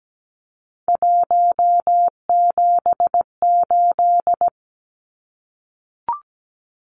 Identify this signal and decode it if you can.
Morse code, heard as a keyed tone, about 17 wpm, 700 Hz. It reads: 178
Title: Numbers - 3-Digit 17wpm